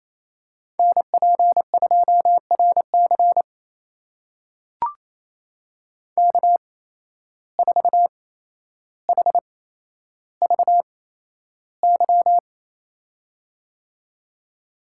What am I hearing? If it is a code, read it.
Morse code, heard as a keyed tone, about 28 wpm, spaced as for 12 wpm, 700 Hz. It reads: NP2RC K4HVY
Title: USA Call Signs - 5 Characters - 8x Character Spacing - 28wpm